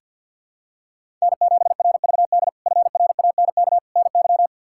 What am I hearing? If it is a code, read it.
Morse code, heard as a keyed tone, about 49 wpm, 700 Hz. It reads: N7RVD FRANK NY
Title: K1USN SST QSO Practice - 50wpm